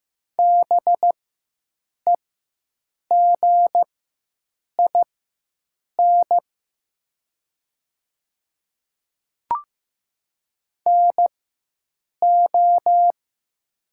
Morse code, heard as a keyed tone, about 15 wpm, spaced as for 7 wpm, 700 Hz. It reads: BEGIN NO
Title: 4x Character Spacing - Top 300 Words - 15wpm